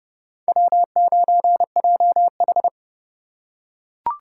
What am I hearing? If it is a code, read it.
Morse code, heard as a keyed tone, about 30 wpm, 700 Hz. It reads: W9JH